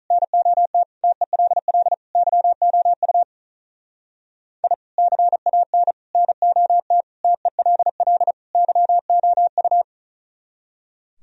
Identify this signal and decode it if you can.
Morse code, heard as a keyed tone, about 45 wpm, 700 Hz. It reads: NOT TELL YOU I CAN NOT TELL YOU